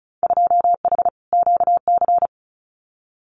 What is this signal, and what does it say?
Morse code, heard as a keyed tone, about 36 wpm, 700 Hz. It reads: 2H QC
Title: Winter Field Day Exchanges 35wpm